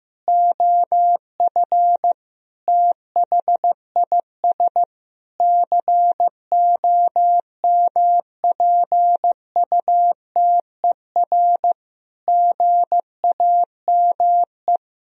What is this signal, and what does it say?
Morse code, heard as a keyed tone, about 15 wpm, 700 Hz. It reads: OF THIS COMPUTER GAME
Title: Sentences from Top 1000 Words - Review Part 3 - 15wpm